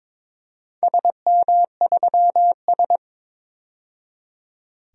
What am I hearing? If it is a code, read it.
Morse code, heard as a keyed tone, about 22 wpm, 700 Hz. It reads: SM3S
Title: DXCC-100 Call Signs - 4 Characters - 22wpm